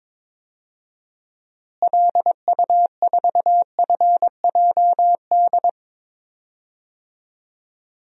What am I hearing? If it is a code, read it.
Morse code, heard as a keyed tone, about 22 wpm, 700 Hz. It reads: LU4FJD
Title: DXCC-100 Call Signs - 6 Characters - Repeated 3x - 22wpm